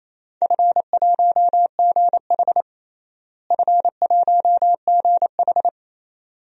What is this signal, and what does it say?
Morse code, heard as a keyed tone, about 27 wpm, 700 Hz. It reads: F1GH F1GH
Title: DXCC-100 Call Signs - 4 Characters - Repeated 2x - 28wpm